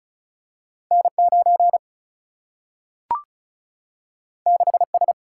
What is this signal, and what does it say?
Morse code, heard as a keyed tone, about 36 wpm, 700 Hz. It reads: N 9 6 S
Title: Single-Letter Single-Number 35wpm